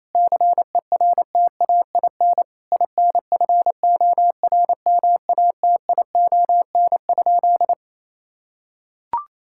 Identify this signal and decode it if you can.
Morse code, heard as a keyed tone, about 28 wpm, 700 Hz. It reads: CERTAIN INFORMATION?
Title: Sentences from Top 1000 Words - Review Part 6 - 28wpm